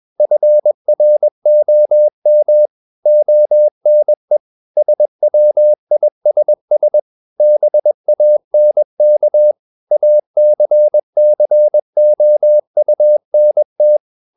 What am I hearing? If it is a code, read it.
Morse code, heard as a keyed tone, about 21 wpm, 600 Hz. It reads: FROM ONE SWISS BANK ACCOUNT
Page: CW med Gnister